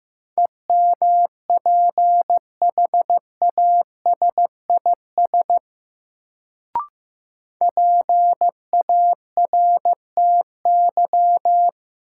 Morse code, heard as a keyed tone, about 15 wpm, 700 Hz. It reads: EMPHASIS PARTY
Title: Sentences from Top 1400 Words - New Word Drill - 15wpm